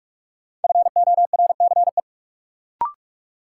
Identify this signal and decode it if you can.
Morse code, heard as a keyed tone, about 42 wpm, 700 Hz. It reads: FORCE
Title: Sentences from Top 700 Words - New Word Drill - 45wpm